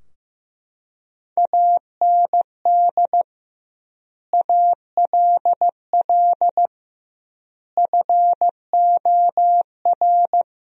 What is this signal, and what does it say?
Morse code, heard as a keyed tone, about 15 wpm, 700 Hz. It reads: AND ALL FOR
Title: 2x Word-Spacing - Sentences from Top 100 Words - 15wpm